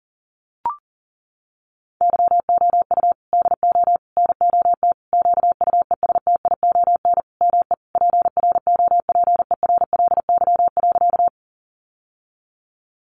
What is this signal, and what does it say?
Morse code, heard as a keyed tone, about 40 wpm, 700 Hz. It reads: YOU DO NOT QUESTION ME PROPERLY.
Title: Sentences from Top 1100 Words 40wpm